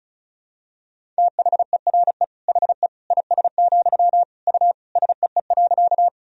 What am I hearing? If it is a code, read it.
Morse code, heard as a keyed tone, about 34 wpm, 700 Hz. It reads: THERE HE IS, U SEE.
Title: Sentences from Top 100 Words and Abbreviations 35wpm